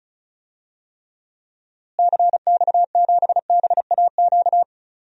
Morse code, heard as a keyed tone, about 34 wpm, 700 Hz. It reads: CX7BAQ
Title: DXCC-100 Call Signs - 6 Characters - Repeated 2x - 35wpm